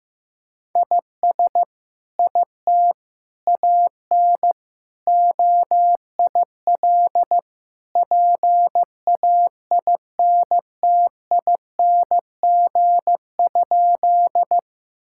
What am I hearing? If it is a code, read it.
Morse code, heard as a keyed tone, about 14 wpm, 700 Hz. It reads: IS IT AN OIL PAINTING?
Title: Sentences from Top 1000 Words - Review Part 6 - 15wpm